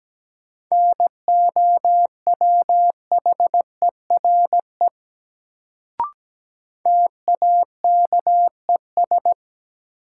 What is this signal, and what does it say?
Morse code, heard as a keyed tone, about 17 wpm, 700 Hz. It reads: NOWHERE TAKES
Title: Sets of 5 Words Encore - New Word Drill - 17wpm